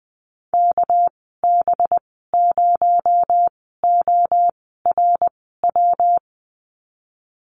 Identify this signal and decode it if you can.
Morse code, heard as a keyed tone, about 20 wpm, spaced as for 14 wpm, 700 Hz. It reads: KB0ORW
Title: 219 Lesson 35g - Number 0 - Call Signs 20wpm